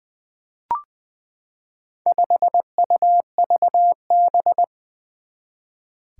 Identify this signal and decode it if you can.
Morse code, heard as a keyed tone, about 20 wpm, 700 Hz. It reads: 5UVB
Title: Four Challenging Letter-Number Combinations 20wpm